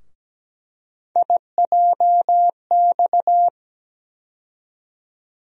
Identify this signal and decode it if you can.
Morse code, heard as a keyed tone, about 17 wpm, 700 Hz. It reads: IJX
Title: USA Call Sign Postfixes - 3 Letters 17wpm